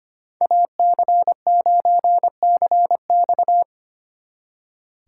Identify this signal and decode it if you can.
Morse code, heard as a keyed tone, about 31 wpm, 700 Hz. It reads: AC9CX
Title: Speed Racing - USA 2x2 Call Signs 25wpm